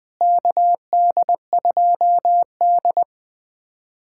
Morse code, heard as a keyed tone, about 21 wpm, 700 Hz. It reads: KD2D
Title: USA Callsign DE Callsign - 4 Characters - 20wpm